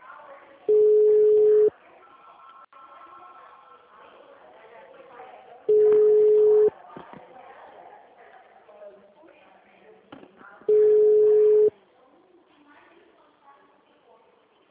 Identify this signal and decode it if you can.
Ringback tone (Europe, 425 Hz), with speech.
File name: out- -2014